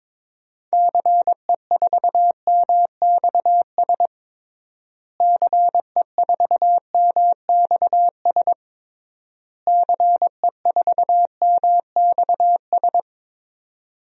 Morse code, heard as a keyed tone, about 22 wpm, 700 Hz. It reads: CE4MXS CE4MXS CE4MXS
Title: DXCC-100 Call Signs - 6 Characters - Repeated 3x - 22wpm